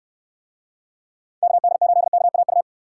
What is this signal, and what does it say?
Morse code, heard as a keyed tone, about 69 wpm, 700 Hz. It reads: KD8KAU